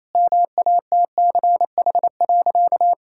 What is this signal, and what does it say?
Morse code, heard as a keyed tone, about 27 wpm, 700 Hz. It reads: MATCH.
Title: Sentences from Top 1600 Words 28wpm